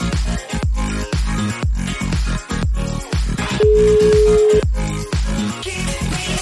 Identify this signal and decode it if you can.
Ringback tone (Europe, 425 Hz), with noise.